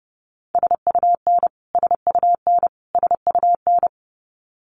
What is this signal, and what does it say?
Morse code, heard as a keyed tone, about 30 wpm, 700 Hz. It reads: SUN SUN SUN
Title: Top 500 Words Repeated 30wpm